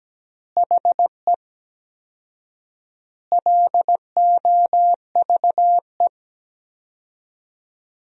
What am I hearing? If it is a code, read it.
Morse code, heard as a keyed tone, about 17 wpm, 700 Hz. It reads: HE LOVE
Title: 4x Word-Spacing - Sentences from Top 400 Words - 17wpm